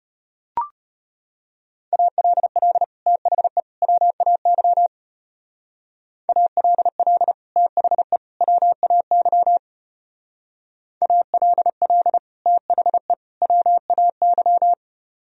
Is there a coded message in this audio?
Yes: Morse code from 1.9 to 14.7 s, heard as a keyed tone, about 32 wpm, 700 Hz. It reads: ALL THE WAY ALL THE WAY ALL THE WAY